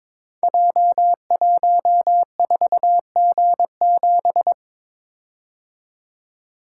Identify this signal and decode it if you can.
Morse code, heard as a keyed tone, about 22 wpm, 700 Hz. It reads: J14G7
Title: Code Groups - 5 Letter:Numbers - 22wpm